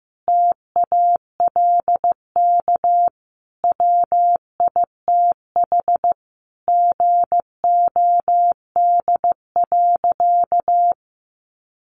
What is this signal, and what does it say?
Morse code, heard as a keyed tone, about 15 wpm, 700 Hz. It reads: TALK WITH GOD.
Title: A Sound Of Thunder 15wpm